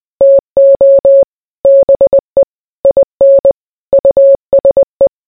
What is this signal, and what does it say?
Morse code, heard as a keyed tone, about 20 wpm, 550 Hz. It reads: TO BE IN USE